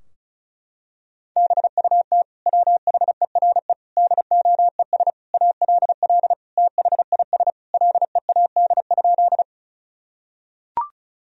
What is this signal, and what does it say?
Morse code, heard as a keyed tone, about 34 wpm, 700 Hz. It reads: BUT WHERE DOES ALL THIS LEAD?
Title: Sentences from Top 1000 Words - Review Part 2 - 35wpm